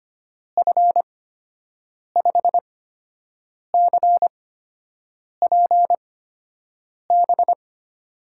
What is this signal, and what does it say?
Morse code, heard as a keyed tone, about 25 wpm, spaced as for 7 wpm, 700 Hz. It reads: F5CPB
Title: USA Call Signs - 6 Characters - 8x Character Spacing - 25wpm